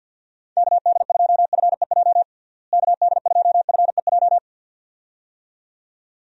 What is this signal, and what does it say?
Morse code, heard as a keyed tone, about 52 wpm, 700 Hz. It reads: KD2FEJ KD2FEJ
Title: USA Call Signs - 6 Characters - Repeated 2x - 50wpm